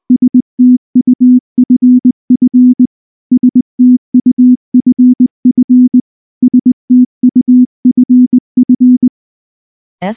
Morse code, heard as a keyed tone, about 20 wpm, 250 Hz. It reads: STUFF STUFF STUFF
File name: morse / morse-training-20WPM-9